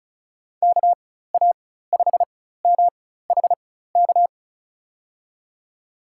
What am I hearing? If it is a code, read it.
Morse code, heard as a keyed tone, about 36 wpm, spaced as for 16 wpm, 700 Hz. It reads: KA5MHK
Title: USA Call Signs - 6 Characters - 4x Character Spacing - 35wpm